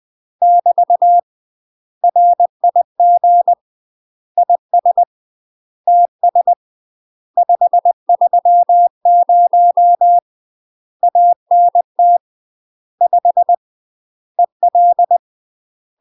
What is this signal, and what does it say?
Morse code, heard as a keyed tone, about 20 wpm, 700 Hz. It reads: <BT> RIG IS TS 530 ANT 5 EL
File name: UK MASTER QSOs TRAINING / 20WPM / UK MASTER QSOs 0003